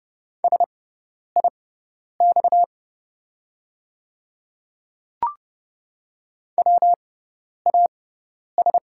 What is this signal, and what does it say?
Morse code, heard as a keyed tone, about 30 wpm, spaced as for 10 wpm, 690 Hz. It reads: SIX WAS
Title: 6x Character Spacing - Top 500 Words - 30wpm